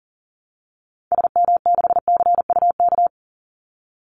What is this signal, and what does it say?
Morse code, heard as a keyed tone, about 40 wpm, 700 Hz. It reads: SM6CUK
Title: Call Signs Repeated 2x - 40wpm